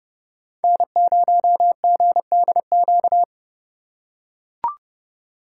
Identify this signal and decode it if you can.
Morse code, heard as a keyed tone, about 29 wpm, 700 Hz. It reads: N0GDQ